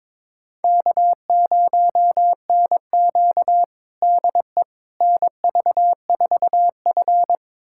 Morse code, heard as a keyed tone, about 21 wpm, 700 Hz. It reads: K0NQ DE NV4F
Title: USA Callsign DE Callsign - 4 Characters - 22wpm